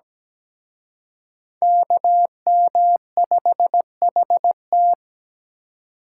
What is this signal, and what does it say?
Morse code, heard as a keyed tone, about 20 wpm, 700 Hz. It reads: KM5HT
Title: Speed Racing - USA 2x2 Call Signs 17wpm